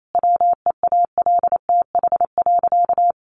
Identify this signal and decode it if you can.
Morse code, heard as a keyed tone, about 28 wpm, 700 Hz. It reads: WEALTH.
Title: Sentences from Top 1300 Words 28wpm